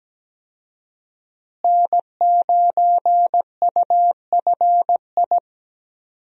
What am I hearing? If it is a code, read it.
Morse code, heard as a keyed tone, about 17 wpm, 700 Hz. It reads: N9UFI
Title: USA Call Signs - 5 Characters - Repeated 3x - 17wpm